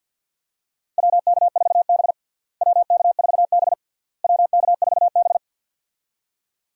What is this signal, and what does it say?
Morse code, heard as a keyed tone, about 50 wpm, 700 Hz. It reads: WK4B WK4B WK4B
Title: USA Call Signs - 4 Characters - Repeated 3x - 50wpm